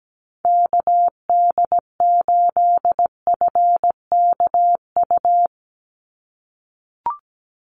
Morse code, heard as a keyed tone, about 17 wpm, 700 Hz. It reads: KD8FKU
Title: USA Call Signs 17wpm